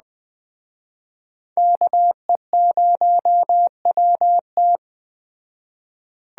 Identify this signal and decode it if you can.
Morse code, heard as a keyed tone, about 20 wpm, 700 Hz. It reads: KE0WT
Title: Speed Racing - USA 2x2 Call Signs 17wpm